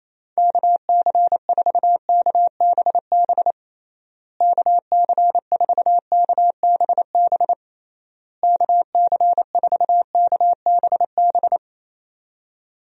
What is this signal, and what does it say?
Morse code, heard as a keyed tone, about 28 wpm, 700 Hz. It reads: KC4KBB KC4KBB KC4KBB
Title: USA Call Signs - 6 Characters - Repeated 3x - 28wpm